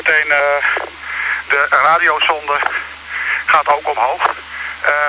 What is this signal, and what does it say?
Emergency Alert System (SAME) tones, heard as harsh buzzing bursts, with speech. It is heard from 1.0 to 5.1 s.